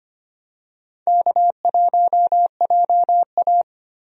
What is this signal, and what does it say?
Morse code, heard as a keyed tone, about 25 wpm, 700 Hz. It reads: K1JA
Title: USA Call Signs - 4 Characters - 25wpm